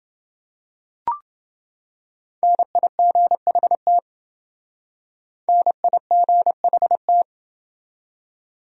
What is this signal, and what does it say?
Morse code, heard as a keyed tone, about 28 wpm, 700 Hz. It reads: NIGHT NIGHT